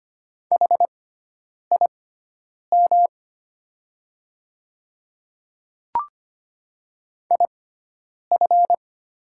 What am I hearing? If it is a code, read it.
Morse code, heard as a keyed tone, about 25 wpm, spaced as for 9 wpm, 700 Hz. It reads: HIM IF
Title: 6x Character Spacing - Top 100 Words - 25wpm